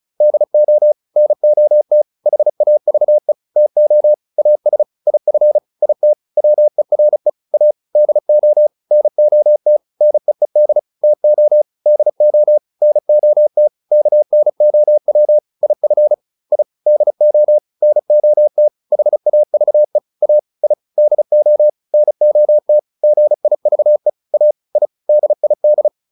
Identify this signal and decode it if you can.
Morse code, heard as a keyed tone, about 34 wpm, 600 Hz. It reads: DO NOT HAVE TO AS IF IT WERE A DO NOT NEED TO DO NOT KNOW IF I DO NOT HAVE A I DO NOT GIVE A I DID